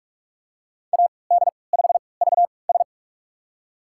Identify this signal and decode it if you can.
Morse code, heard as a keyed tone, about 46 wpm, spaced as for 26 wpm, 700 Hz. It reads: AD5VS
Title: USA Call Signs - 5 Characters - 3x Character Spacing - 45wpm